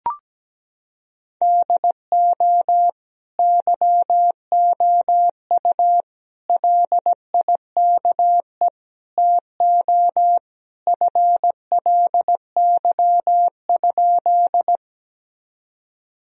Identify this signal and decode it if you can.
Morse code, heard as a keyed tone, about 17 wpm, 700 Hz. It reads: DO YOU LIKE TO FLY?
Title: Sentences from Top 1000 Words - Review Part 2 - 17wpm